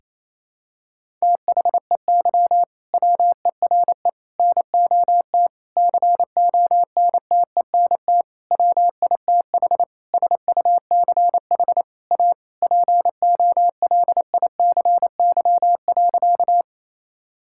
Morse code, heard as a keyed tone, about 28 wpm, 700 Hz. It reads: THEY WERE NOT CONTENT WITH SUCH A POLICY.
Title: Sentences from Top 1000 Words - Review Part 6 - 28wpm